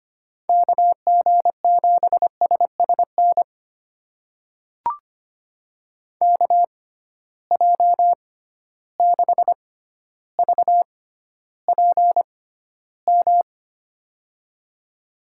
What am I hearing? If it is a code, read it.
Morse code, heard as a keyed tone, about 25 wpm, spaced as for 13 wpm, 700 Hz. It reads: KG7SSN KJ6VPM
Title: USA Call Signs - 6 Characters - 6x Character Spacing - 25wpm